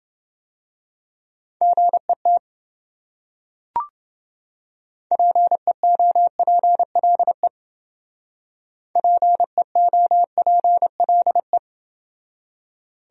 Morse code, heard as a keyed tone, about 29 wpm, 700 Hz. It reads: GET PEOPLE PEOPLE